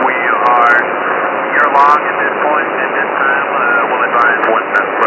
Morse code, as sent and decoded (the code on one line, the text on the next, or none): none